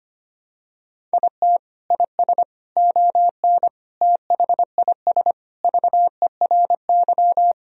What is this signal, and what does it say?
Morse code, heard as a keyed tone, about 25 wpm, 700 Hz. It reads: IT IS ON THIS VERY
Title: Sentences from Top 1000 Words - Review Part 1 - 25wpm